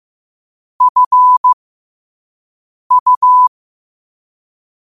Morse code, heard as a keyed tone, about 15 wpm, spaced as for 5 wpm, 1000 Hz. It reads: FU
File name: qotdfiles / 5wpm